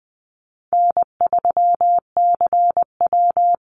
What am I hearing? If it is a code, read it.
Morse code, heard as a keyed tone, about 20 wpm, 700 Hz. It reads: N3CW
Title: Easy Call Signs Repeated 3x - 20wpm